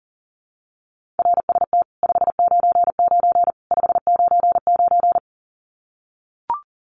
Morse code, heard as a keyed tone, about 40 wpm, 700 Hz. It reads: RST 599 599
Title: Common QSO Phrases 40wpm